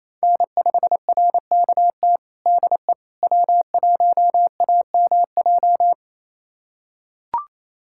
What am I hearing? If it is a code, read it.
Morse code, heard as a keyed tone, about 28 wpm, 700 Hz. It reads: N5RKT DE W1AMJ
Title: USA Callsign DE Callsign - 5 Characters - 28wpm